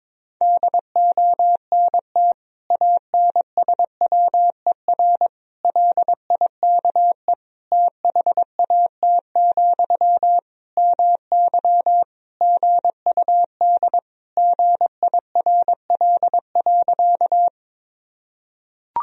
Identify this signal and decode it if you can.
Morse code, heard as a keyed tone, about 22 wpm, 700 Hz. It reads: DONT ANSWER LIKE THAT, MY GUD GIRL.
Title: Sentences from Top 400 Words and Abbreviations 22wpm